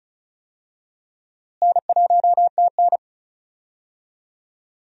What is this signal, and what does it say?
Morse code, heard as a keyed tone, about 34 wpm, 700 Hz. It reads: N1TN
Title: USA Call Signs - 4 Characters - Repeated 2x - 35wpm